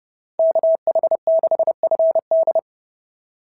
Morse code, heard as a keyed tone, about 30 wpm, 690 Hz. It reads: KH6FD
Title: SOTA QSO Practice 30wpm